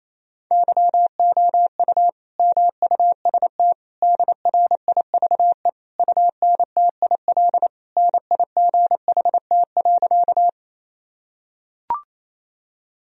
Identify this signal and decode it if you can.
Morse code, heard as a keyed tone, about 28 wpm, 700 Hz. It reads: YOU MUST DRIVE UNTIL NIGHT.
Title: Sentences from Top 1000 Words - Review Part 2 - 28wpm